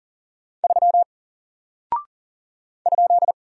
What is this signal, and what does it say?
Morse code, heard as a keyed tone, about 39 wpm, 700 Hz. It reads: 3 ?
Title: Single Character (Letter:Number:Punctuation:Prosign) 40wpm